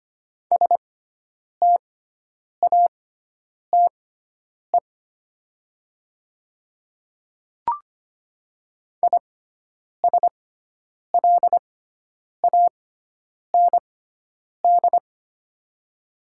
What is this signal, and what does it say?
Morse code, heard as a keyed tone, about 25 wpm, spaced as for 9 wpm, 700 Hz. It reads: STATE ISLAND
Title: 6x Character Spacing - Top 500 Words - 25wpm